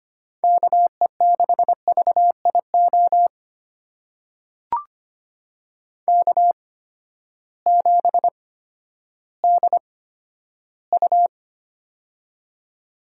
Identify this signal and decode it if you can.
Morse code, heard as a keyed tone, about 25 wpm, spaced as for 12 wpm, 700 Hz. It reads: KE6VIO K7DU
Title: USA Call Signs - 8x Character Spacing - 25wpm